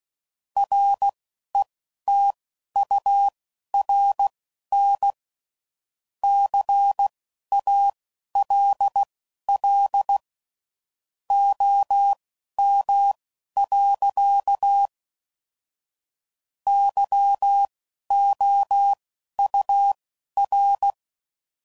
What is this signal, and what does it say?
Morse code, heard as a keyed tone, about 16 wpm, spaced as for 11 wpm, 800 Hz. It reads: RETURN CALL OM. YOUR
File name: CW Code Groups @10WPM / QSOs / QSO kc7haa